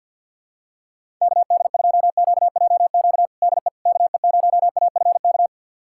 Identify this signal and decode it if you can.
Morse code, heard as a keyed tone, about 49 wpm, 700 Hz. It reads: KD2QJQ DE KE0AFK
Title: USA Callsign DE Callsign 50wpm